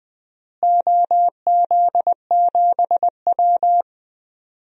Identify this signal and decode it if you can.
Morse code, heard as a keyed tone, about 20 wpm, 700 Hz. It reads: OZ7W
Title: DXCC-100 Call Signs - 4 Characters - Repeated 3x - 20wpm